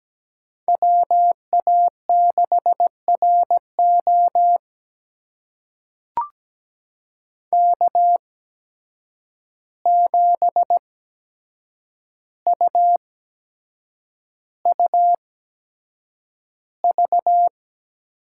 Morse code, heard as a keyed tone, about 17 wpm, spaced as for 7 wpm, 700 Hz. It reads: WA6RO K7UUV
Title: USA Call Signs - 5 Characters - 8x Character Spacing - 17wpm